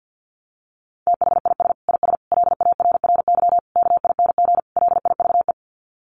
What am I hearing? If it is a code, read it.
Morse code, heard as a keyed tone, about 50 wpm, 700 Hz. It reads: THIS IS LARRY KING LIVE
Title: Sets of 5 Words Encore2 50wpm